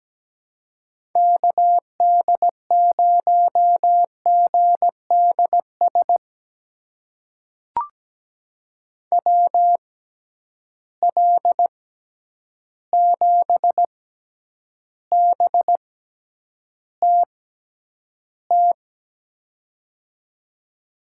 Morse code, heard as a keyed tone, about 17 wpm, spaced as for 9 wpm, 700 Hz. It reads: KD0GDS WL7BTT
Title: USA Call Signs - 6 Characters - 6x Character Spacing - 17wpm